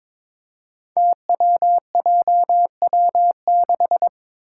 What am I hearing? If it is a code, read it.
Morse code, heard as a keyed tone, about 22 wpm, 700 Hz. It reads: TWJW6
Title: Five Challenging Letter-Number Combinations 22wpm